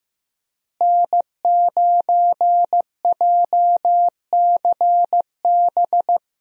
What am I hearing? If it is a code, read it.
Morse code, heard as a keyed tone, about 16 wpm, 700 Hz. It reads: N9JCB
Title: USA Call Signs - 5 Characters - Repeated 3x - 15wpm